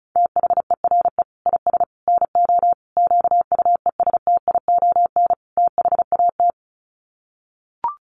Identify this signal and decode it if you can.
Morse code, heard as a keyed tone, about 34 wpm, 700 Hz. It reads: THERE IS NO QUESTION THAT
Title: Sets of 5 Words Encore 35wpm